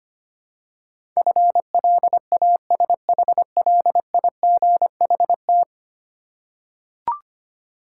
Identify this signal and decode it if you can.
Morse code, heard as a keyed tone, about 25 wpm, 700 Hz. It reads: FLASHLIGHT
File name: practice / 25wpm / Easy Words with Hints - 25wpm